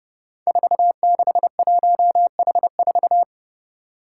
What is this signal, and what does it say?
Morse code, heard as a keyed tone, about 28 wpm, 700 Hz. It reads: 461H4
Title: Five Challenging Letter-Number Combinations 30wpm